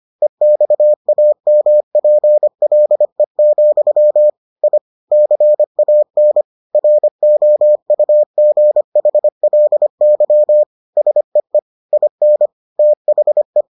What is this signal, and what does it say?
Morse code, heard as a keyed tone, about 25 wpm, 600 Hz. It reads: EXAMPLE, I CAN ROUGHLY SEE IN THE